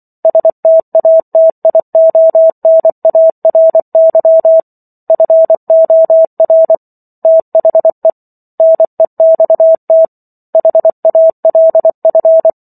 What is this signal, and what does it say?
Morse code, heard as a keyed tone, about 24 wpm, 650 Hz. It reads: STATIONARY FOR THE NEXT HALF
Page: War of the Worlds - 15-Chapter 15 - 24 WPM